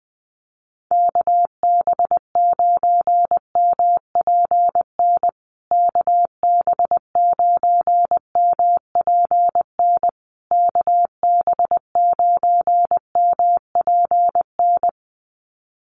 Morse code, heard as a keyed tone, about 20 wpm, 700 Hz. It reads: KB9MPN KB9MPN KB9MPN
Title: USA Call Signs Repeated 3x 20wpm